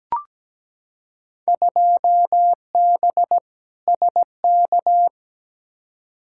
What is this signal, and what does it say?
Morse code, heard as a keyed tone, about 17 wpm, 700 Hz. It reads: 2B SK
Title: Field Day Exchanges Encore - 17wpm